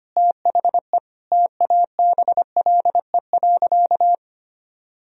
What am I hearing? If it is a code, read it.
Morse code, heard as a keyed tone, about 25 wpm, 700 Hz. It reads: THE TABLE.
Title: Sentences from Top 1000 Words - Review Part 2 - 25wpm